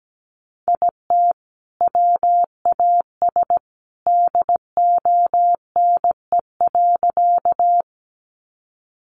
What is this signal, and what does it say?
Morse code, heard as a keyed tone, about 17 wpm, 700 Hz. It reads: IT WAS DONE.
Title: Sentences from Top 500 Words 17wpm